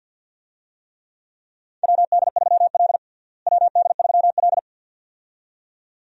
Morse code, heard as a keyed tone, about 48 wpm, 700 Hz. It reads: WD3L WD3L
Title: USA Call Signs - 4 Characters - Repeated 2x - 50wpm